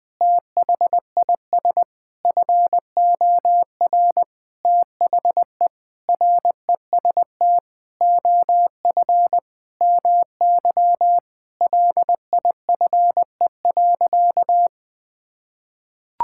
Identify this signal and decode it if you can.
Morse code, heard as a keyed tone, about 20 wpm, 700 Hz. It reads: THIS FOR THE REST OF MY LIFE.
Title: Sentences with 5-Word Phrases - 20wpm